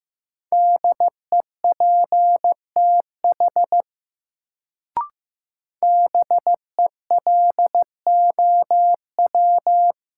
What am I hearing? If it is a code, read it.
Morse code, heard as a keyed tone, about 15 wpm, 700 Hz. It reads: DEPTH BELOW
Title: Sentences from Top 1200 Words - New Word Drill Alt - 15wpm